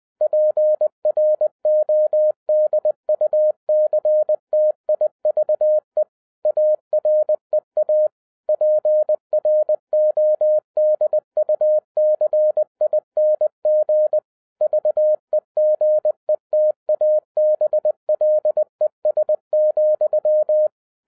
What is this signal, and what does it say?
Morse code, heard as a keyed tone, about 20 wpm, 610 Hz. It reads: PRODUCTIVE AREA PRODUCING VEGETABLES,